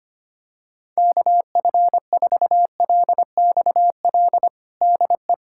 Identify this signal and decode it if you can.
Morse code, heard as a keyed tone, about 25 wpm, 700 Hz. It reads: KF4LXL DE
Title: USA Callsign DE Callsign - 6 Characters - 25wpm